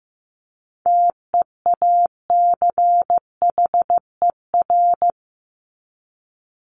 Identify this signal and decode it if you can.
Morse code, heard as a keyed tone, about 15 wpm, 700 Hz. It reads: TEACHER
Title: Top 751-1000 Words 15wpm